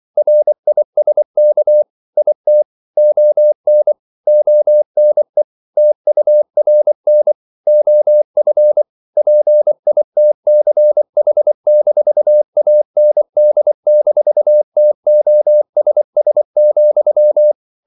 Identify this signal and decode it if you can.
Morse code, heard as a keyed tone, about 24 wpm, 600 Hz. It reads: RISK IT ON ONE TURN OF PITCH-AND-TOSS,
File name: CW / Practice / If24